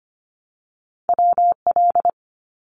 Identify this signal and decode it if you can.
Morse code, heard as a keyed tone, about 25 wpm, 700 Hz. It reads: WL
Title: Abbreviations 25wpm